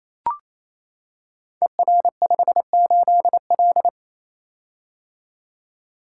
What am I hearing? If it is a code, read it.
Morse code, heard as a keyed tone, about 27 wpm, 700 Hz. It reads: ER58L
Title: Code Groups - 5 Letter:Numbers - 28wpm